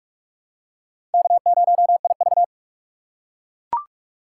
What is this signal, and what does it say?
Morse code, heard as a keyed tone, about 38 wpm, 700 Hz. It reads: K0IV